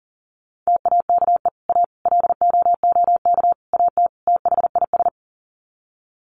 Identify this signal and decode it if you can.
Morse code, heard as a keyed tone, about 40 wpm, 700 Hz. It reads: TAKE A LOOK AT THIS
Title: Sets of 5 Words 40wpm